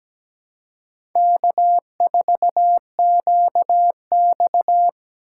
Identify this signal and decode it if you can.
Morse code, heard as a keyed tone, about 20 wpm, 700 Hz. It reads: K4QX
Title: Speed Racing - USA 1x2 Call Signs 17wpm